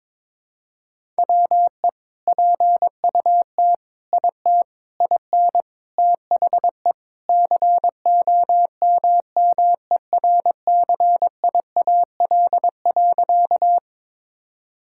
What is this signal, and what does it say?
Morse code, heard as a keyed tone, about 22 wpm, 700 Hz. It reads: WE PUT IT IN THE COMMERCIAL.
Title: Sentences from Top 1000 Words - Review Part 5 - 22wpm